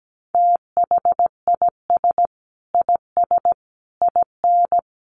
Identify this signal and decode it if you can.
Morse code, heard as a keyed tone, about 17 wpm, 700 Hz. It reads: THIS IS IN
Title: Sentences from Top 1200 Words 17wpm